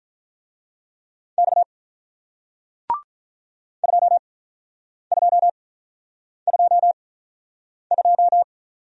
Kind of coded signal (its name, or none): Morse code